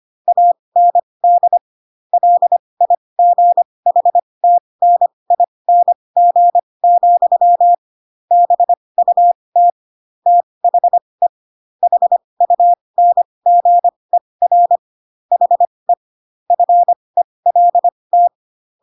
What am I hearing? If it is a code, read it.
Morse code, heard as a keyed tone, about 25 wpm, spaced as for 20 wpm, 700 Hz. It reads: AND LIGHTNING, BUT THE HUNGER HE FELT
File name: Pinocchio / Pinocchio ch6 15 25 ik1hge